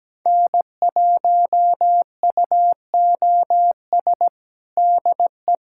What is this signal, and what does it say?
Morse code, heard as a keyed tone, about 17 wpm, 700 Hz. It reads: N1UOS DE
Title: USA Callsign DE Callsign 17wpm